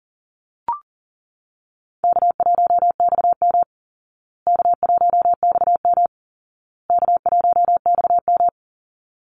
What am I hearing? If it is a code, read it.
Morse code, heard as a keyed tone, about 40 wpm, 700 Hz. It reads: K1XM K1XM K1XM
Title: Easy Call Signs Repeated 3x - 40wpm